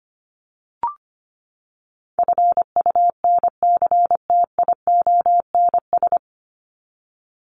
Morse code, heard as a keyed tone, about 25 wpm, 700 Hz. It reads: FUNCTIONS
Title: ES:S Suffix - Top 100 Words with Base Words 25wpm